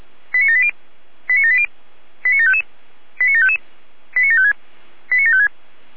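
Five-tone selective call (ZVEI), heard as quick runs of five tones, tones of about 70 ms. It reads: 89780 89780 89755 89755 89756 89756
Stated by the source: ZVEI five-tone selective call 89706, received by SDR on the 455 channel.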